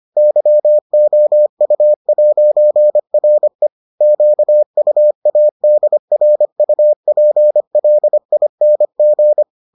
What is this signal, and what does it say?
Morse code, heard as a keyed tone, about 25 wpm, 600 Hz. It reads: YOU'RE QUADRUPLING